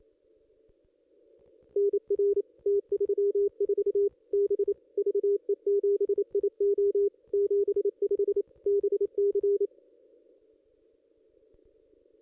Morse code, heard as a keyed tone, about 28 wpm, 400 Hz. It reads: NR T34 B VE7IO 75 BC
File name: contests / ss-cw-k5tr-op-n5rz -1900